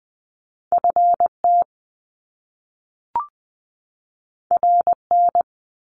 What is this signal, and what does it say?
Morse code, heard as a keyed tone, about 19 wpm, 700 Hz. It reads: FT RN
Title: Top Two Letter Combinations in Words 20wpm